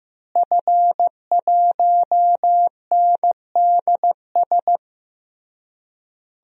Morse code, heard as a keyed tone, about 15 wpm, 700 Hz. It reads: F1NDS
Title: DXCC-100 Call Signs - 5 Characters - Repeated 3x - 15wpm